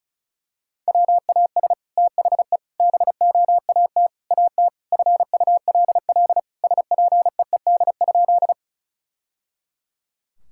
Morse code, heard as a keyed tone, about 34 wpm, 700 Hz. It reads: WAS THE BOAT AT FULL SPEED?
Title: Sentences from Top 1000 Words - Review Part 5 - 35wpm